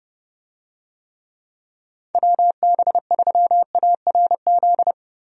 Morse code, heard as a keyed tone, about 32 wpm, 700 Hz. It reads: WB3ARZ
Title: USA Call Signs - 6 Characters - 30wpm